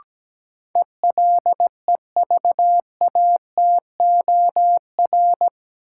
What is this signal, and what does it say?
Morse code, heard as a keyed tone, about 17 wpm, 700 Hz. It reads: ELEVATOR
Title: Sentences from Top 1600 Words - New Word Drill - 17wpm